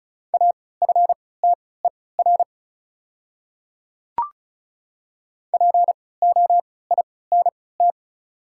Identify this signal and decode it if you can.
Morse code, heard as a keyed tone, about 36 wpm, spaced as for 20 wpm, 690 Hz. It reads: AFTER POINT
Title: 3x Character Spacing - Top 300 Words - 35wpm